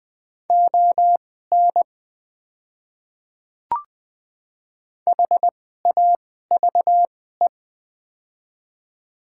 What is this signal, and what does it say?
Morse code, heard as a keyed tone, about 20 wpm, spaced as for 14 wpm, 700 Hz. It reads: ON HAVE
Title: 2x Character Spacing - Top 100 Words - 20wpm